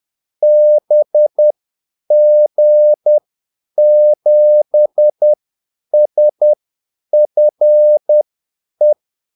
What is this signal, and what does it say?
Morse code, heard as a keyed tone, about 10 wpm, spaced as for 8 wpm, 600 Hz. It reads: BG7SFE
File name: Callsigns-10-8wpm